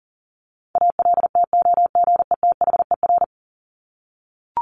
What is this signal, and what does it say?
Morse code, heard as a keyed tone, about 40 wpm, 700 Hz. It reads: ALTOGETHER
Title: Top 1501-1750 Words 40wpm